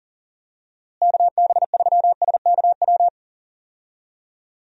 Morse code, heard as a keyed tone, about 37 wpm, 700 Hz. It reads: KB3SKW
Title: USA Call Signs - 6 Characters - Repeated 3x - 40wpm